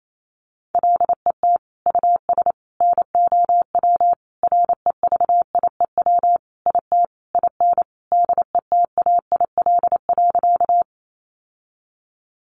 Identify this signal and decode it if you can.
Morse code, heard as a keyed tone, about 28 wpm, 700 Hz. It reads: LET US NOW REVIEW IT IN DETAIL.
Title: Sentences from Top 1500 Words 28wpm